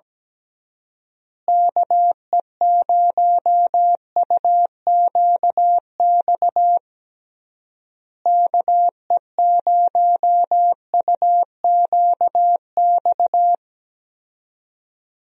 Morse code, heard as a keyed tone, about 17 wpm, 700 Hz. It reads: KE0UQX KE0UQX
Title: USA Call Signs - 6 Characters - Repeated 3x - 17wpm